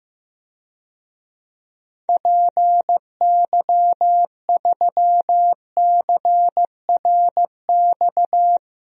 Morse code, heard as a keyed tone, about 15 wpm, 700 Hz. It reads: PY3CRX
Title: DXCC-100 Call Signs - 6 Characters - Repeated 3x - 15wpm